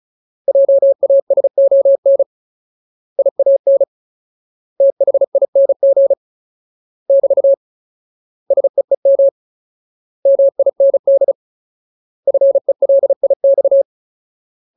Morse code, heard as a keyed tone, about 34 wpm, 550 Hz. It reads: JASON IAN THING <BT> SEEM MIND FELIX
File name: Ultimate Word List / 35WPM Ultimate